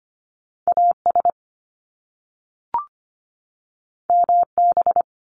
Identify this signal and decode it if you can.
Morse code, heard as a keyed tone, about 25 wpm, 700 Hz. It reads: AS MB
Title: Top Two Letter Combinations in Words 25wpm